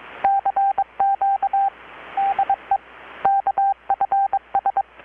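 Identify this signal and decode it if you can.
Morse code, heard as a keyed tone, about 22 wpm, 790 Hz, with radio noise. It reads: CQ DE KFS
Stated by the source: Morse from KPH's Night of Nights XX maritime radiotelegraphy event, received in Chiapas, Mexico about 3,800 km away on a Kenwood R-600 with a grounded 10 m long-wire antenna.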